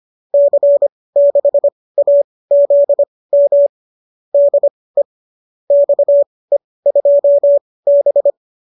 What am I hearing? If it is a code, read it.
Morse code, heard as a keyed tone, about 25 wpm, spaced as for 18 wpm, 590 Hz. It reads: C6AZM DE XE2B
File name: qso204 18